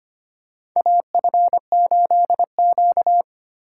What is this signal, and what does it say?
Morse code, heard as a keyed tone, about 25 wpm, 700 Hz. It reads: AF8Q
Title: USA Call Signs - 4 Characters - 3x Character Spacing - 25wpm